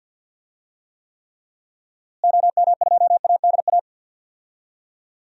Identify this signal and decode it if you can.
Morse code, heard as a keyed tone, about 49 wpm, 700 Hz. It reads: OM2ADA
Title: DXCC-100 Call Signs - 6 Characters - Repeated 3x - 50wpm